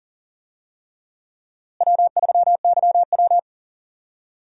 Morse code, heard as a keyed tone, about 39 wpm, 700 Hz. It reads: W3YW
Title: USA Call Signs - 4 Characters - Repeated 3x - 40wpm